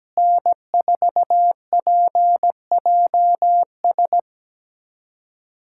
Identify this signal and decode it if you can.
Morse code, heard as a keyed tone, about 17 wpm, 700 Hz. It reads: N4PJS
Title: USA Callsign DE Callsign 17wpm